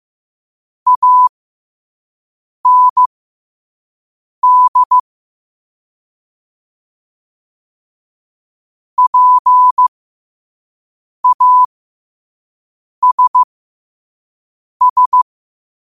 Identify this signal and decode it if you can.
Morse code, heard as a keyed tone, about 15 wpm, spaced as for 5 wpm, 1000 Hz. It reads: AND PASS
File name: qotdfiles / 5wpm